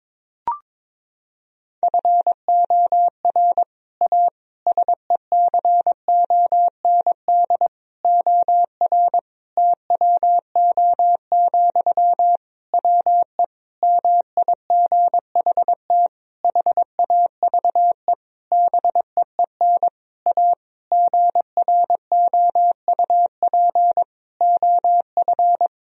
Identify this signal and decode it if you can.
Morse code, heard as a keyed tone, about 22 wpm, 700 Hz. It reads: FOR A SECOND OR TWO, WE MIGHT HAVE BEEN A GROUP OF
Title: Sentences from Top 1000 Words - Review Part 1 - 22wpm